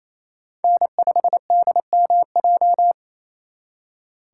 Morse code, heard as a keyed tone, about 32 wpm, 700 Hz. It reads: N5DMJ
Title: Speed Racing - USA 1x3 Call Signs 28wpm